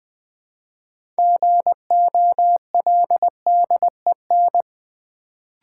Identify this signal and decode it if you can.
Morse code, heard as a keyed tone, about 20 wpm, 700 Hz. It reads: GOLDEN
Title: Build A Word - 2 Letters At A Time - 6-Letter Words - 20wpm